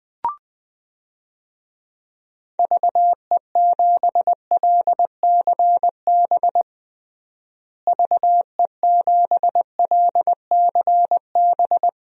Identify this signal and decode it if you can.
Morse code, heard as a keyed tone, about 20 wpm, 700 Hz. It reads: VE7LCB VE7LCB
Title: DXCC-100 Call Signs - 6 Characters - Repeated 3x - 20wpm